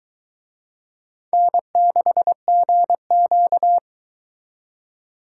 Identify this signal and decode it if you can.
Morse code, heard as a keyed tone, about 22 wpm, 700 Hz. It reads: N6GQ